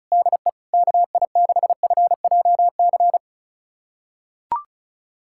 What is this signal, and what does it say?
Morse code, heard as a keyed tone, about 34 wpm, 700 Hz. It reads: DE KI6FJC
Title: USA Callsign DE Callsign - 6 Characters - 35wpm